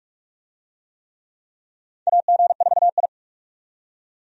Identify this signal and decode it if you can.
Morse code, heard as a keyed tone, about 42 wpm, 700 Hz. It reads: AG4I
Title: USA Call Signs - 4 Characters - 45wpm